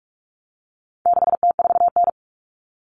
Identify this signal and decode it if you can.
Morse code, heard as a keyed tone, about 44 wpm, 700 Hz. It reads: 6T4N
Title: Common Cut Numbers - 4-Digit 45wpm